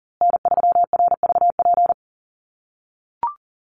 Morse code, heard as a keyed tone, about 40 wpm, 700 Hz. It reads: N3RVP
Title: USA Call Signs Repeated 2x 40wpm